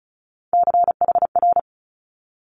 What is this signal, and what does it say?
Morse code, heard as a keyed tone, about 36 wpm, 700 Hz. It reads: CHR
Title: Multi-letter Phonemes 35wpm